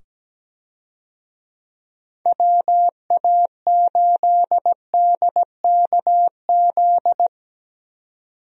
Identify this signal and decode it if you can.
Morse code, heard as a keyed tone, about 17 wpm, 700 Hz. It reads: WA8DKZ
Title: USA Call Signs - 6 Characters - 17wpm